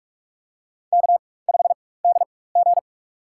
Morse code, heard as a keyed tone, about 46 wpm, spaced as for 21 wpm, 700 Hz. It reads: K5DG
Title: USA Call Signs - 4 Characters - 4x Character Spacing - 45wpm